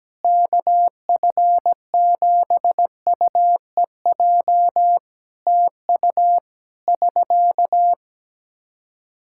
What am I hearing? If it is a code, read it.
Morse code, heard as a keyed tone, about 17 wpm, 700 Hz. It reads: KF7UEJ TU <SK>
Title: Practice QSO - Exchange 4 - Ending - 17wpm